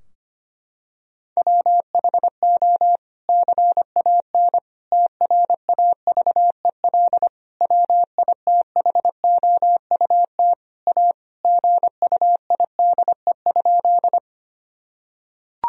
Morse code, heard as a keyed tone, about 25 wpm, 700 Hz. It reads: WHO CAN TRAVEL WITHOUT A GUIDE?
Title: Sentences from Top 1000 Words - Review Part 3 - 25wpm